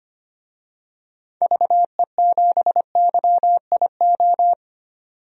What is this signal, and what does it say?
Morse code, heard as a keyed tone, about 25 wpm, 700 Hz. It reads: VE7YIO
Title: DXCC-100 Call Signs - 6 Characters - 25wpm